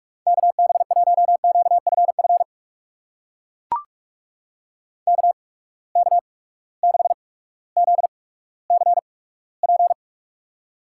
Morse code, heard as a keyed tone, about 41 wpm, spaced as for 19 wpm, 700 Hz. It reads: KB1QFF KK6ZCP
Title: USA Call Signs - 6 Characters - 8x Character Spacing - 45wpm